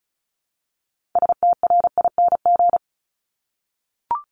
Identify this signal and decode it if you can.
Morse code, heard as a keyed tone, about 36 wpm, 700 Hz. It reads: STRING
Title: Multi-letter Phonemes 35wpm